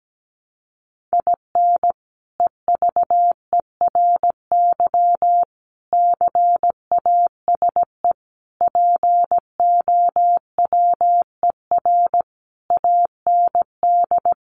Morse code, heard as a keyed tone, about 17 wpm, 700 Hz. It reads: IN EVERY CASE POWER AND
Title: Sentences from Top 1200 Words 17wpm